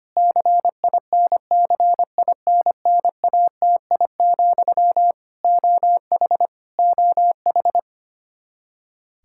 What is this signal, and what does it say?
Morse code, heard as a keyed tone, about 25 wpm, 700 Hz. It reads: CINCINNATI, OH OH
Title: K1USN SST QSO Practice 25wpm